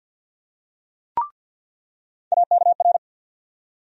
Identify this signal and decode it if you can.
Morse code, heard as a keyed tone, about 47 wpm, 700 Hz. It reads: AKR
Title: USA Call Sign Postfixes - 3 Letters 50wpm